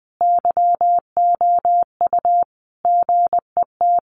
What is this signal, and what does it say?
Morse code, heard as a keyed tone, about 20 wpm, 700 Hz. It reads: YOU GET
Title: Sentences from Top 900 Words 20wpm